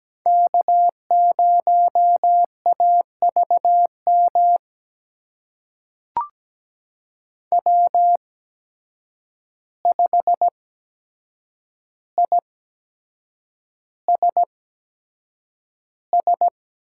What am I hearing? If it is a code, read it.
Morse code, heard as a keyed tone, about 17 wpm, spaced as for 7 wpm, 700 Hz. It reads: K0AVM W5ISS
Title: USA Call Signs - 5 Characters - 8x Character Spacing - 17wpm